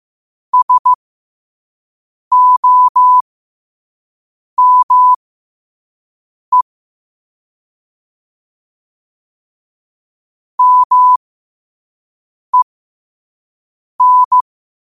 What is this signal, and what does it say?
Morse code, heard as a keyed tone, about 15 wpm, spaced as for 5 wpm, 1000 Hz. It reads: SOME MEN
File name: qotdfiles / 5wpm